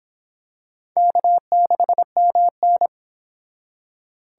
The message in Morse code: K6MN

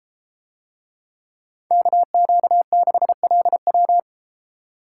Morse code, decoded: KQ6LW

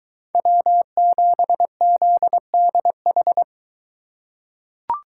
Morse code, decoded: W7ZDH

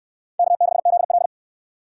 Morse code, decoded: K6ZL

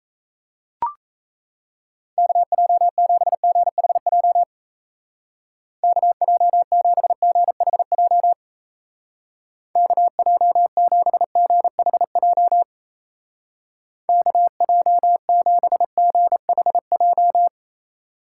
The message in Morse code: KJ7GHJ KJ7GHJ KJ7GHJ KJ7GHJ

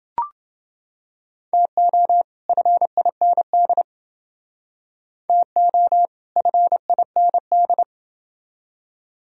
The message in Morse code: TO FIND TO FIND

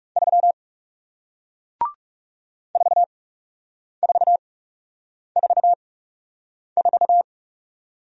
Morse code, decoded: 3 4444

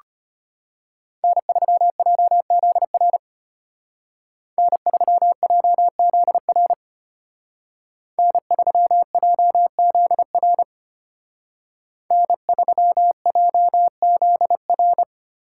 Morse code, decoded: N3JZR N3JZR N3JZR N3JZR

